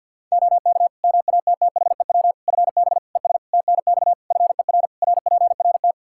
Morse code, heard as a keyed tone, about 51 wpm, 700 Hz. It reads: OK MATTHEW FB ES TNX FER RPRT
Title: Practice QSO - Exchange 2 - Info - 50wpm